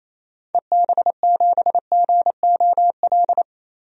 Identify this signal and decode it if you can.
Morse code, heard as a keyed tone, about 28 wpm, 700 Hz. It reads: EB7GOL
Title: DXCC-100 Call Signs - 6 Characters - 28wpm